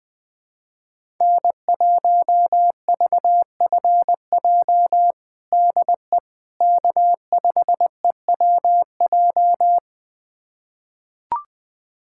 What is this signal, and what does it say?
Morse code, heard as a keyed tone, about 21 wpm, 700 Hz. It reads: N1VFJ DE K5EWJ
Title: USA Callsign DE Callsign - 5 Characters - 20wpm